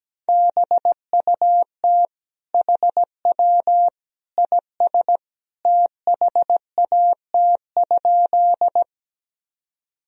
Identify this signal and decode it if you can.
Morse code, heard as a keyed tone, about 17 wpm, 700 Hz. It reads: BUT HW IS THAT?
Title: Sentences from Top 100 Words and Abbreviations 17wpm